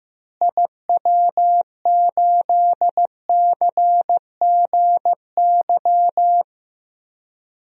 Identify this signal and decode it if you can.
Morse code, heard as a keyed tone, about 16 wpm, 700 Hz. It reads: IW8CGY
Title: DXCC-100 Call Signs - 6 Characters - Repeated 3x - 15wpm